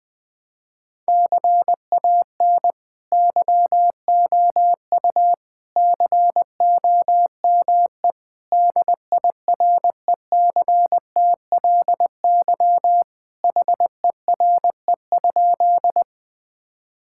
Morse code, heard as a keyed tone, about 20 wpm, 700 Hz. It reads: CAN YOU COME DIRECTLY HERE?
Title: Sentences from Top 1000 Words - Review Part 5 - 20wpm